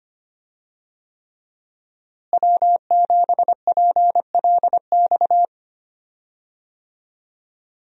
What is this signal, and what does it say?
Morse code, heard as a keyed tone, about 25 wpm, 700 Hz. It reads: W7PLX
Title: USA Call Signs - 5 Characters - Repeated 3x - 25wpm